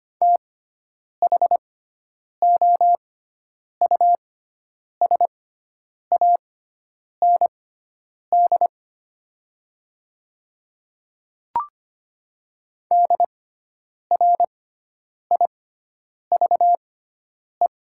Morse code, heard as a keyed tone, about 25 wpm, spaced as for 9 wpm, 700 Hz. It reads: THOUSAND DRIVE
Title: 6x Character Spacing - Top 500 Words - 25wpm